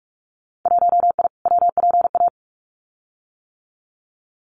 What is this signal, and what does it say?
Morse code, heard as a keyed tone, about 46 wpm, 700 Hz. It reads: 1I WPA
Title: Winter Field Day Exchanges Encore 45wpm